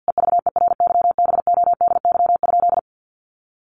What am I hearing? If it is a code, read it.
Morse code, heard as a keyed tone, about 48 wpm, 700 Hz. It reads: EVERYBODY?
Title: Sentences from Top 1400 Words 50wpm